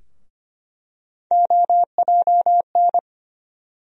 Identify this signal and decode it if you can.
Morse code, heard as a keyed tone, about 25 wpm, 700 Hz. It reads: OJN
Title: USA Call Sign Postfixes - 3 Letters 25wpm